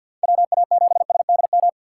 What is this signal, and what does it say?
Morse code, heard as a keyed tone, about 52 wpm, 690 Hz. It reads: WA7SDM
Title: USA Call Signs - 2x Character Spacing - 50wpm